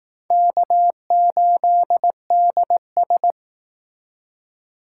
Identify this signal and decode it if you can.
Morse code, heard as a keyed tone, about 19 wpm, 700 Hz. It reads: K8DS